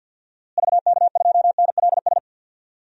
Morse code, heard as a keyed tone, about 51 wpm, 700 Hz. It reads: VK2NLS
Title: DXCC-100 Call Signs - 6 Characters - Repeated 3x - 50wpm